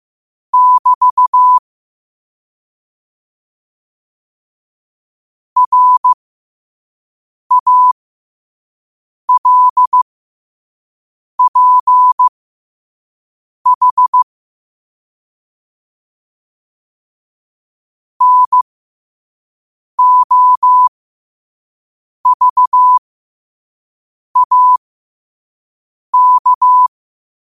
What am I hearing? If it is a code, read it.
Morse code, heard as a keyed tone, about 15 wpm, spaced as for 5 wpm, 1000 Hz. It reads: <BT> RALPH NOVAK
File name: qotdfiles / 5wpm